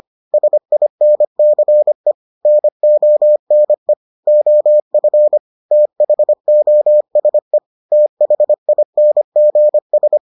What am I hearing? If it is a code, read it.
Morse code, heard as a keyed tone, about 25 wpm, 600 Hz. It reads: SINCE NONE OF THOSE THINGS